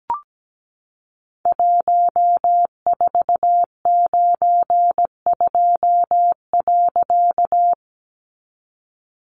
Morse code, heard as a keyed tone, about 17 wpm, 700 Hz. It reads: 1492.
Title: A Sound Of Thunder 17wpm